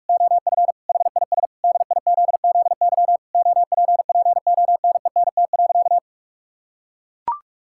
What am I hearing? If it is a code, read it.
Morse code, heard as a keyed tone, about 41 wpm, 700 Hz. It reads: OF HIS DIZZY OPPONENT.
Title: Pangram Challenge v2 - 45wpm